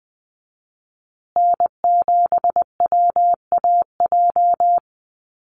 Morse code, heard as a keyed tone, about 20 wpm, 700 Hz. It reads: N7WAJ
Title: USA Call Signs Repeated 2x 20wpm